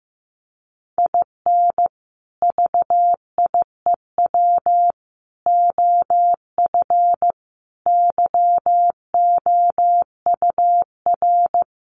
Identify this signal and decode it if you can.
Morse code, heard as a keyed tone, about 15 wpm, 700 Hz. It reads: IN VIEW OF YOUR
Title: Sentences from Top 900 Words 15wpm